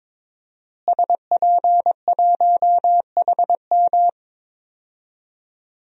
Morse code, heard as a keyed tone, about 22 wpm, 700 Hz. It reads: SP1HM
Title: DXCC-100 Call Signs - 5 Characters - Repeated 2x - 22wpm